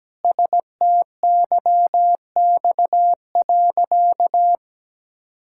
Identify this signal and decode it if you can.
Morse code, heard as a keyed tone, about 17 wpm, 700 Hz. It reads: STYX.
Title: Pangram Challenge v2 - 17wpm